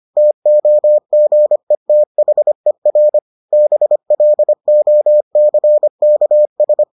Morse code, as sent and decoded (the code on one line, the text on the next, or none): - --- --. - .... .-. / -... .-.. --- -.-. -.- ...
TOGETHER BLOCKS